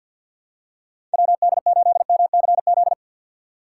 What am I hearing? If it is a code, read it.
Morse code, heard as a keyed tone, about 53 wpm, 700 Hz. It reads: WD8MCZ
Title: USA Call Signs - 6 Characters - 6x Character Spacing - 50wpm